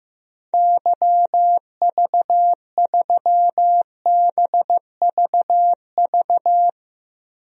Morse code, heard as a keyed tone, about 15 wpm, 700 Hz. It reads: YV3BVV
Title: DXCC-100 Call Signs - 6 Characters - Repeated 3x - 15wpm